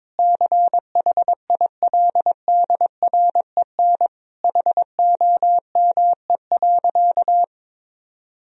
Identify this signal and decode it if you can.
Morse code, heard as a keyed tone, about 21 wpm, 700 Hz. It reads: CHILDREN HOME.
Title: Sentences from Top 1000 Words - Review Part 3 - 22wpm